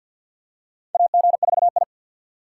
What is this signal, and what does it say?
Morse code, heard as a keyed tone, about 47 wpm, 690 Hz. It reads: AG4I
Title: USA Call Signs - 4 Characters - 6x Character Spacing - 50wpm